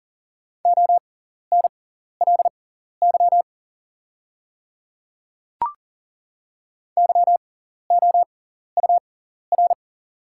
Morse code, heard as a keyed tone, about 40 wpm, spaced as for 14 wpm, 700 Hz. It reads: ONLY YOUR
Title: 6x Character Spacing - Top 300 Words - 40wpm